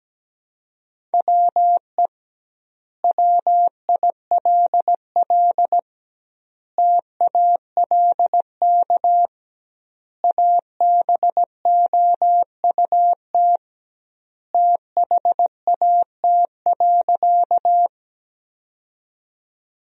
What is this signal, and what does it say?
Morse code, heard as a keyed tone, about 17 wpm, 700 Hz. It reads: WE WILL TALK ABOUT THAT.
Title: 2x Word-Spacing - Sentences from Top 400 Words - 17wpm